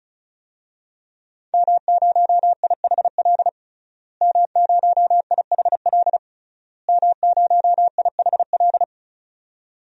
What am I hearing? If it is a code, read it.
Morse code, heard as a keyed tone, about 36 wpm, 700 Hz. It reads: M0IHL M0IHL M0IHL
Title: DXCC-100 Call Signs - 5 Characters - Repeated 3x - 35wpm